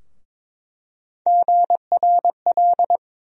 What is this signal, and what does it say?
Morse code, heard as a keyed tone, about 22 wpm, 700 Hz. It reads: GRL
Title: USA Call Sign Postfixes - 3 Letters 22wpm